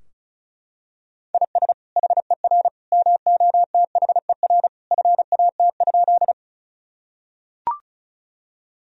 Morse code, heard as a keyed tone, about 34 wpm, 700 Hz. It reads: IS HER MOTHER FAT?
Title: Sentences from Top 1000 Words - Review Part 4 - 35wpm